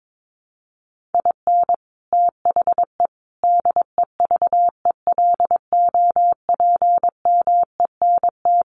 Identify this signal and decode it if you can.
Morse code, heard as a keyed tone, about 22 wpm, 700 Hz. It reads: IN THE DEVELOPMENT
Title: Sets of 5 Words Encore2 22wpm